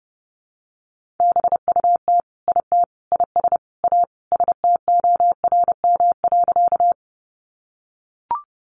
Morse code, heard as a keyed tone, about 30 wpm, 700 Hz. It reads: BUT IT IS A STORM.
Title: Sentences from Top 1500 Words 30wpm